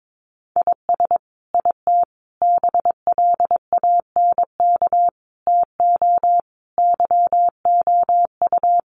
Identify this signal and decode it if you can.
Morse code, heard as a keyed tone, about 22 wpm, 700 Hz. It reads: IS IT BLANK TO YOU
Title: Sentences from Top 1500 Words 22wpm